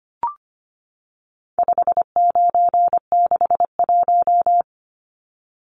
Morse code, heard as a keyed tone, about 25 wpm, 700 Hz. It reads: 5961
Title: Numbers - 4-Digit 25wpm